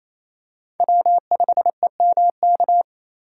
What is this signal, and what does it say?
Morse code, heard as a keyed tone, about 28 wpm, 700 Hz. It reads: W5EMK
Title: USA Call Signs - 5 Characters - 2x Character Spacing - 28wpm